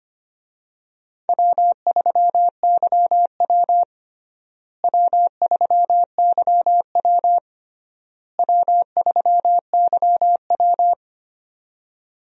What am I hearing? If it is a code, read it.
Morse code, heard as a keyed tone, about 25 wpm, 700 Hz. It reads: W3YW W3YW W3YW
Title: USA Call Signs - 4 Characters - Repeated 3x - 25wpm